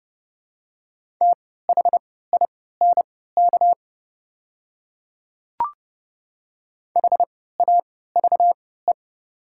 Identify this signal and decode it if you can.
Morse code, heard as a keyed tone, about 32 wpm, spaced as for 17 wpm, 700 Hz. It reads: THINK HAVE
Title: 3x Character Spacing - Top 300 Words - 30wpm